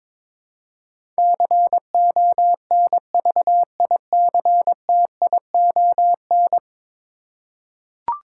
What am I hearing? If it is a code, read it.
Morse code, heard as a keyed tone, about 21 wpm, 700 Hz. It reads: CONVICTION
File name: practice / 22wpm / 10 Letter Words - First 3 Letters Given - 22wpm